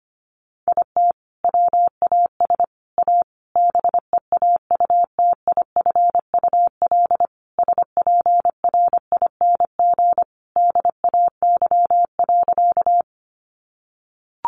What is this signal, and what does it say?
Morse code, heard as a keyed tone, about 25 wpm, 700 Hz. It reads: IT WAS A BEAUTIFUL SPRING DAY.
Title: Sentences from Top 1200 Words 25wpm